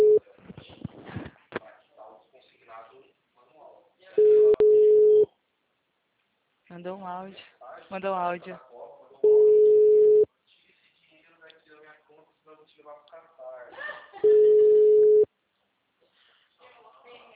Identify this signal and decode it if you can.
Ringback tone (Europe, 425 Hz), with speech.